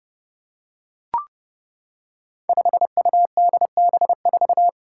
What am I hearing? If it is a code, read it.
Morse code, heard as a keyed tone, about 31 wpm, 700 Hz. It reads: 5UDB4
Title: Five Challenging Letter-Number Combinations 30wpm